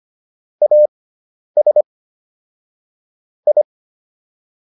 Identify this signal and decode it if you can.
Morse code, heard as a keyed tone, about 25 wpm, spaced as for 10 wpm, 600 Hz. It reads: AS I